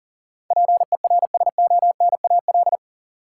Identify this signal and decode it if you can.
Morse code, heard as a keyed tone, about 40 wpm, 700 Hz. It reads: PERSONAL
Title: Sentences from Top 700 Words - New Word Drill - 40wpm